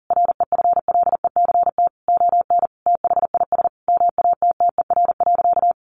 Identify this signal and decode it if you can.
Morse code, heard as a keyed tone, about 40 wpm, 700 Hz. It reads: REFLECT ON THIS MATTER.
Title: Sentences from Top 1500 Words 40wpm